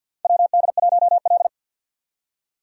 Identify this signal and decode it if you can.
Morse code, heard as a keyed tone, about 50 wpm, 700 Hz. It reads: WD1L
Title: USA Call Signs - 4 Characters - 3x Character Spacing - 50wpm